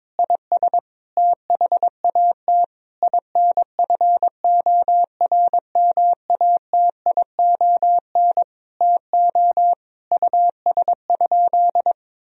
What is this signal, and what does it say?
Morse code, heard as a keyed tone, about 22 wpm, 700 Hz. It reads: IS THAT INFORMATION TO US?
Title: Sentences from Top 1000 Words - Review Part 3 - 22wpm